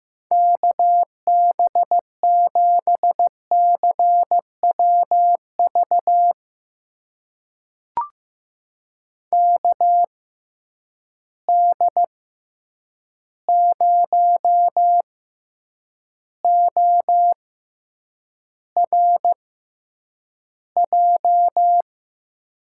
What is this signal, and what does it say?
Morse code, heard as a keyed tone, about 15 wpm, spaced as for 8 wpm, 700 Hz. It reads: KB7CWV KD0ORJ
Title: USA Call Signs - 6 Characters - 6x Character Spacing - 15wpm